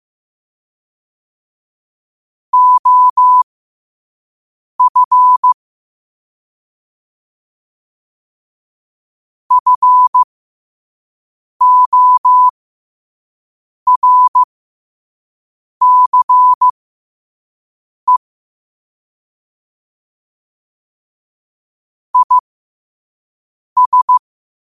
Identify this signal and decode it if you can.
Morse code, heard as a keyed tone, about 15 wpm, spaced as for 5 wpm, 1000 Hz. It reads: OF FORCE IS
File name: qotdfiles / 5wpm